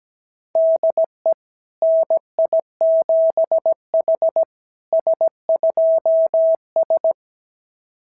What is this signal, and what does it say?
Morse code, heard as a keyed tone, about 17 wpm, 660 Hz. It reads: DE NI7H S2S
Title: SOTA QSO Practice 17wpm